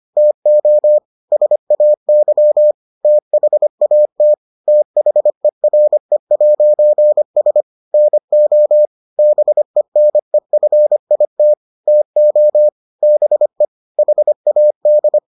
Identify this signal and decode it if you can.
Morse code, heard as a keyed tone, about 25 wpm, 600 Hz. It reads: TO SAY THAT THERE'S NO BENEFIT TO BE HAD